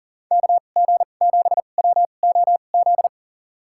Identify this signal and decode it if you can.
Morse code, heard as a keyed tone, about 40 wpm, spaced as for 29 wpm, 700 Hz. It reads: KG7WOZ
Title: USA Call Signs - 6 Characters - 2x Character Spacing - 40wpm